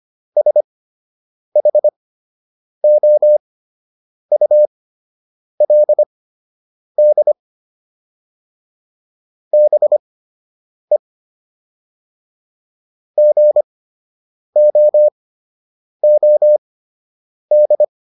Morse code, heard as a keyed tone, about 25 wpm, spaced as for 8 wpm, 600 Hz. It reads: SHOULD BE GOOD